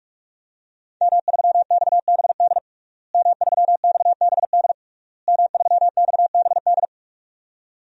Morse code, heard as a keyed tone, about 41 wpm, 700 Hz. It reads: M3XBD M3XBD M3XBD
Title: DXCC-100 Call Signs - 5 Characters - Repeated 3x - 45wpm